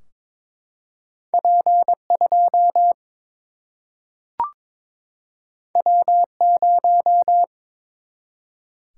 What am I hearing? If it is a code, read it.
Morse code, heard as a keyed tone, about 22 wpm, 700 Hz. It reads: P 2 W 0
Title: Single-Letter Single-Number 22wpm